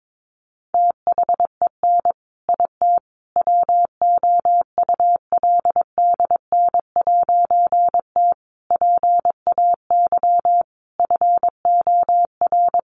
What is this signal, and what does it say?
Morse code, heard as a keyed tone, about 22 wpm, 700 Hz. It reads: THEN IT WOULDN'T PAY FOR
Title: A Sound Of Thunder 22wpm